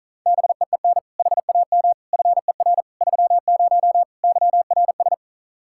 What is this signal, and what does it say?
Morse code, heard as a keyed tone, about 43 wpm, 700 Hz. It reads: BEEN HAM FER 30 YRS
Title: Speed-Racing Common QSO Phrases - 35wpm